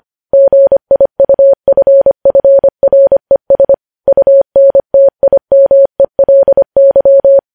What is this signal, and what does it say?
Morse code, heard as a keyed tone, about 25 wpm, 550 Hz. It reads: GIUFFRES UNTIMELY
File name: podcast / News headlines-1x-25x25